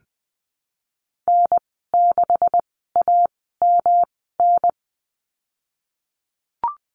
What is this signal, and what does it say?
Morse code, heard as a keyed tone, about 20 wpm, spaced as for 15 wpm, 700 Hz. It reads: N6AMN
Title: 173 Lesson 28g - Number 9 - Call Signs 20wpm